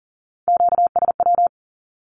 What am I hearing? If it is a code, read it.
Morse code, heard as a keyed tone, about 40 wpm, 700 Hz. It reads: QSW
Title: Q-Codes 40wpm